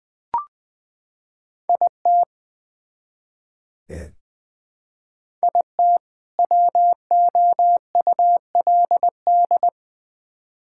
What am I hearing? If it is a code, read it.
Morse code, heard as a keyed tone, about 20 wpm, 700 Hz. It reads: IT IT WOULD
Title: Build A Phrase - 2-Word Phrases - 20wpm